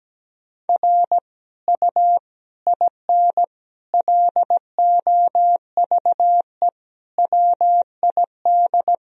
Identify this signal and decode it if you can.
Morse code, heard as a keyed tone, about 17 wpm, 700 Hz. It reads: R U IN LOVE WID
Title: Sentences from Top 400 Words and Abbreviations 17wpm